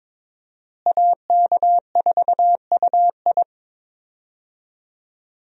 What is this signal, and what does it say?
Morse code, heard as a keyed tone, about 22 wpm, 700 Hz. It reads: AK4UI
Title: USA Call Signs - 5 Characters - Repeated 3x - 22wpm